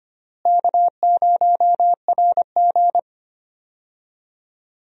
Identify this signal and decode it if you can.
Morse code, heard as a keyed tone, about 25 wpm, 700 Hz. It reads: K0RG
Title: USA Call Signs - 4 Characters - Repeated 2x - 25wpm